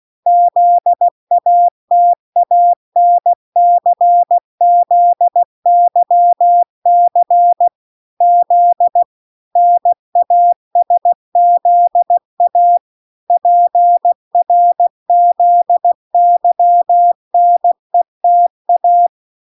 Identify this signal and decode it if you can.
Morse code, heard as a keyed tone, about 16 wpm, 700 Hz. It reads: ZATANCZYC Z NASZA PRZYNETA